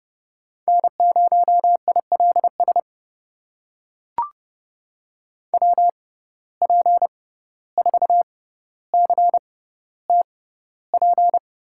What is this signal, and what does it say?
Morse code, heard as a keyed tone, about 30 wpm, spaced as for 15 wpm, 700 Hz. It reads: N0ILS WP4CTP
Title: USA Call Signs - 6x Character Spacing - 30wpm